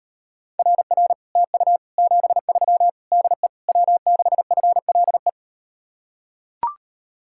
Morse code, heard as a keyed tone, about 38 wpm, 700 Hz. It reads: RR TU 73 DE W6FLE